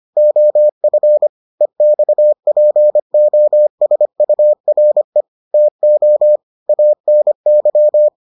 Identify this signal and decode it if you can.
Morse code, heard as a keyed tone, about 25 wpm, 600 Hz. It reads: OF EXPOSURE TO ANY